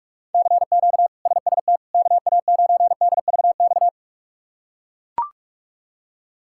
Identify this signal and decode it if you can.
Morse code, heard as a keyed tone, about 42 wpm, 700 Hz. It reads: CQ SST KA9DVX
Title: K1USN SST QSO Practice - 45wpm